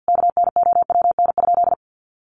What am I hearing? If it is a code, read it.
Morse code, heard as a keyed tone, about 50 wpm, 700 Hz. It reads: KNOWN?
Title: Sentences from Top 1200 Words 50wpm